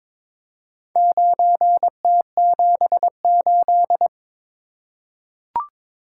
Morse code, heard as a keyed tone, about 22 wpm, 700 Hz. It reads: 9T78
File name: practice / 22wpm / Four Letter-Number Combinations - 22wpm